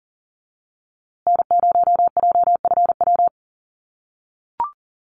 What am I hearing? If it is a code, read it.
Morse code, heard as a keyed tone, about 40 wpm, 700 Hz. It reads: N0JFW
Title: USA Call Signs Repeated 2x 40wpm